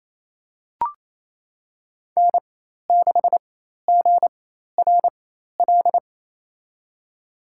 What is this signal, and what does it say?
Morse code, heard as a keyed tone, about 27 wpm, spaced as for 13 wpm, 700 Hz. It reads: N6GRL
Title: USA Call Signs - 5 Characters - 4x Character Spacing - 28wpm